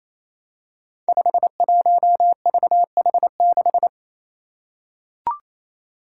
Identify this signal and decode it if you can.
Morse code, heard as a keyed tone, about 28 wpm, 700 Hz. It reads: 51VH6
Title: Five Challenging Letter-Number Combinations 28wpm